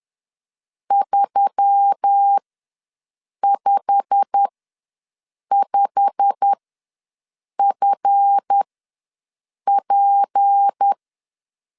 Morse code, heard as a keyed tone, about 10 wpm, spaced as for 6 wpm, 800 Hz. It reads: 355FP
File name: LEZIONI / 8 YJ45 / 8 YJ45 ws 10wpm 3843